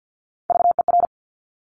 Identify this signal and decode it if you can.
Morse code, heard as a keyed tone, about 50 wpm, 700 Hz. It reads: VER
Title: Top Three Letter Combinations in Words 50wpm